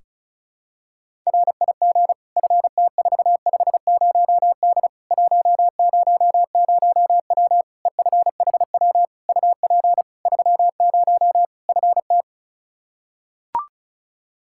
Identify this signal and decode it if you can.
Morse code, heard as a keyed tone, about 34 wpm, 700 Hz. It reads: RIG FT450D 100W EFHW UP 30 FT
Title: K1USN SST QSO Practice 35wpm